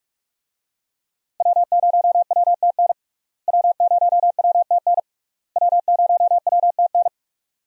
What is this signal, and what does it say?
Morse code, heard as a keyed tone, about 42 wpm, 700 Hz. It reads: W0WTN W0WTN W0WTN
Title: USA Call Signs - 5 Characters - Repeated 3x - 45wpm